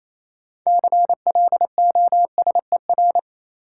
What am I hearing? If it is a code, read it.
Morse code, heard as a keyed tone, about 28 wpm, 700 Hz. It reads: CLOSER
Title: Sets of 5 Words Encore - New Word Drill - 28wpm